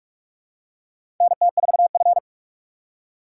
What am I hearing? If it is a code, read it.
Morse code, heard as a keyed tone, about 46 wpm, 700 Hz. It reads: NT4F
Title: USA Call Signs - 4 Characters - 3x Character Spacing - 45wpm